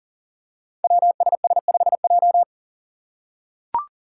Morse code, heard as a keyed tone, about 42 wpm, 700 Hz. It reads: WSS5J
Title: Five Challenging Letter-Number Combinations 40wpm